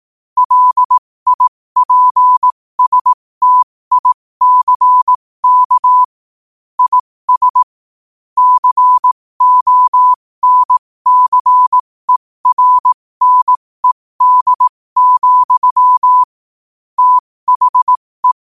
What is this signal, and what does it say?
Morse code, heard as a keyed tone, about 18 wpm, spaced as for 15 wpm, 1000 Hz. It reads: LIPSTICK IS CONCERNED, THE
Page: New quotes every day in morse code at 15 Words per minute.